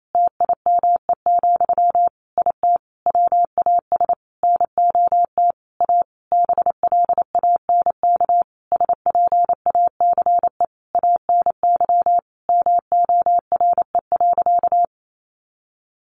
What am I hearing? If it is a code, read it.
Morse code, heard as a keyed tone, about 28 wpm, 700 Hz. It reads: TIME, IT WAS NOT A BLANK SPACE ANY MORE.
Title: Sentences from Top 1500 Words 28wpm